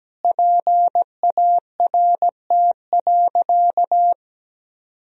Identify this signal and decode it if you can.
Morse code, heard as a keyed tone, about 17 wpm, 700 Hz. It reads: PART.
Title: Sentences from Top 1000 Words - Review Part 1 - 17wpm